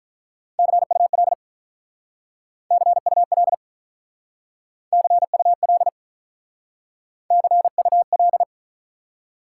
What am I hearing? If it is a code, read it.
Morse code, heard as a keyed tone, about 43 wpm, 700 Hz. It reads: CUL CUL CUL CUL
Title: Speed-Racing Top 90 QSO Elements 35wpm